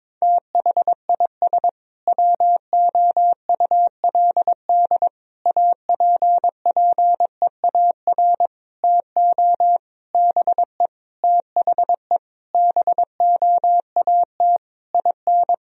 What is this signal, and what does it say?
Morse code, heard as a keyed tone, about 21 wpm, 700 Hz. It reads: THIS WOULD APPEAR TO BE THE BOAT IN
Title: Sentences from Top 1000 Words - Review Part 3 - 22wpm